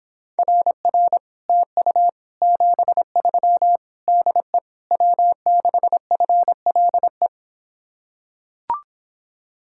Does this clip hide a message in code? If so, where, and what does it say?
Yes: Morse code from 0.4 to 7.3 s, heard as a keyed tone, about 26 wpm, 700 Hz. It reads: RR TU 73 DE W6FLE